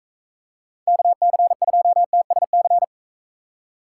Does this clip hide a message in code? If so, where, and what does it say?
Yes: Morse code from 0.9 to 2.8 s, heard as a keyed tone, about 42 wpm, 700 Hz. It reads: KC2TSC